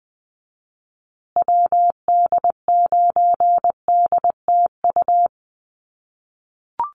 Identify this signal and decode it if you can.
Morse code, heard as a keyed tone, about 20 wpm, 700 Hz. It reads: WD9DTU
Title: USA Call Signs Repeated 3x 20wpm